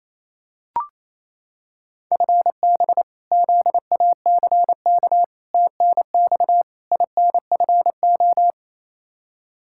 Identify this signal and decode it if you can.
Morse code, heard as a keyed tone, about 28 wpm, 700 Hz. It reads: FB ZACK TNX INFO
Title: Practice QSO - Exchange 3 - More Info - 28wpm